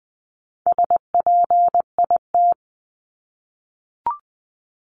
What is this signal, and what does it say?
Morse code, heard as a keyed tone, about 20 wpm, 700 Hz. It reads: SPIT
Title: 260 Lesson 42e - Words 20-20wpm